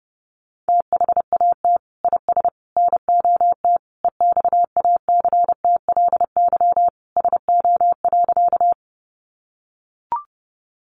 Morse code, heard as a keyed tone, about 30 wpm, 700 Hz. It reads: THAT IS NOT EXACTLY SO.
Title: Sentences from Top 900 Words 30wpm